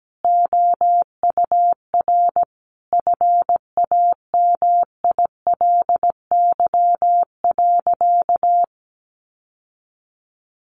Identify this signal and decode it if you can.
Morse code, heard as a keyed tone, about 17 wpm, 700 Hz. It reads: OUR FAMILY.
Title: Sentences from Top 1300 Words 17wpm